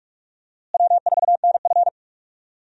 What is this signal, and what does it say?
Morse code, heard as a keyed tone, about 44 wpm, 700 Hz. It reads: W4NF
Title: USA Call Signs - 4 Characters - Repeated 2x - 45wpm